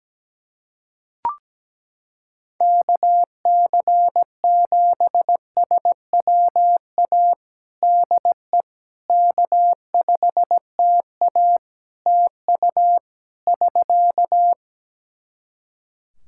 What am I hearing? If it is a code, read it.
Morse code, heard as a keyed tone, about 17 wpm, 700 Hz. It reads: KC7SWA DE K5TA TU <SK>
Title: Practice QSO - Exchange 4 - Ending - 17wpm